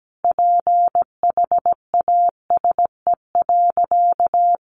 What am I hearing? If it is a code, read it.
Morse code, heard as a keyed tone, about 17 wpm, 700 Hz. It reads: PHASE.
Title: Sentences from Top 1200 Words 17wpm